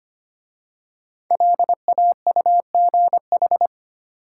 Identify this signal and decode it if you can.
Morse code, heard as a keyed tone, about 25 wpm, 700 Hz. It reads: LAUGH
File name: 5 Letter Words - First 3 Letters Given - 25wpm